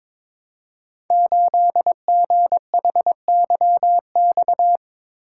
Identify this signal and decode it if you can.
Morse code, heard as a keyed tone, about 22 wpm, 700 Hz. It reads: 8GHYX
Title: Code Groups - 5 Letter:Numbers - 22wpm